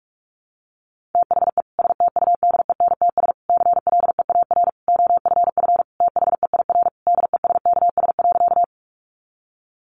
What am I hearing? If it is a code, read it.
Morse code, heard as a keyed tone, about 44 wpm, 700 Hz. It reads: THE STUDENTS CLEAR OFF THEIR DESKS.
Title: Additional Easy Phrases 45wpm